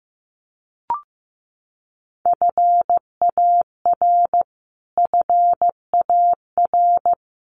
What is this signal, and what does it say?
Morse code, heard as a keyed tone, about 15 wpm, 700 Hz. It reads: FAR FAR
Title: Top 500 Words Repeated 15wpm